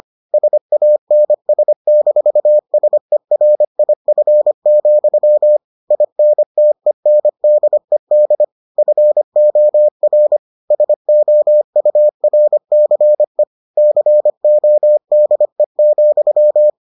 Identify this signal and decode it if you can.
Morse code, heard as a keyed tone, about 25 wpm, 600 Hz. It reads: SANS-SERIF, INTENDED FOR SOURCE CODE,